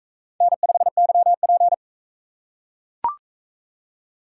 Morse code, heard as a keyed tone, about 43 wpm, 700 Hz. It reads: N5YP